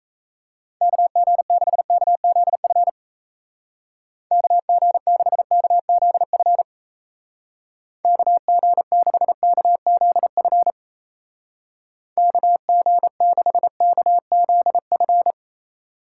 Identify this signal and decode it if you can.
Morse code, heard as a keyed tone, about 34 wpm, 700 Hz. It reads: KG6KZF KG6KZF KG6KZF KG6KZF
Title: Speed Racing - USA Call Signs 28wpm